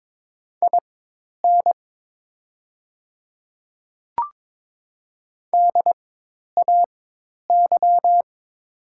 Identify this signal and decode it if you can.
Morse code, heard as a keyed tone, about 22 wpm, spaced as for 10 wpm, 700 Hz. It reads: IN DAY
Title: 4x Character Spacing - Top 100 Words - 22wpm